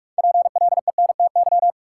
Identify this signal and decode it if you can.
Morse code, heard as a keyed tone, about 42 wpm, 700 Hz. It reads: PLENTY
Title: Sentences from Top 1300 Words - New Word Drill Alt - 45wpm